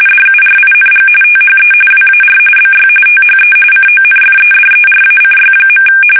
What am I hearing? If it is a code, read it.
RTTY, about 45.45 baud. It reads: DURING THE WAR BY THE CONFEDERACY.
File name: fsk850 / usnavy3